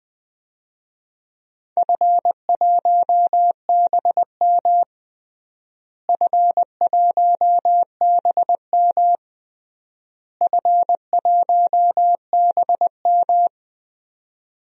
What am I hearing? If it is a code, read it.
Morse code, heard as a keyed tone, about 20 wpm, 700 Hz. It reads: F1BM F1BM F1BM
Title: DXCC-100 Call Signs - 4 Characters - Repeated 3x - 20wpm